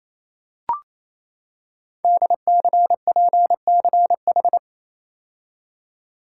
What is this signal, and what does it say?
Morse code, heard as a keyed tone, about 28 wpm, 700 Hz. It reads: DCPCH
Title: Code Groups - 5 Letters - 28wpm